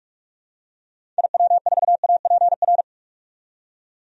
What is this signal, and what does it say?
Morse code, heard as a keyed tone, about 42 wpm, 700 Hz. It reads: IW4APR
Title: DXCC-100 Call Signs - 6 Characters - 45wpm